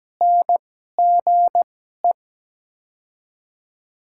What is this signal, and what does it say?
Morse code, heard as a keyed tone, about 17 wpm, spaced as for 12 wpm, 700 Hz. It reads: NGE
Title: 2x Character Spacing - Top 200 Words - 17wpm